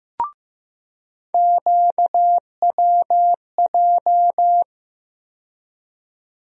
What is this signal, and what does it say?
Morse code, heard as a keyed tone, about 15 wpm, 700 Hz. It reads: QWJ
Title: USA Call Sign Postfixes - 3 Letters 15wpm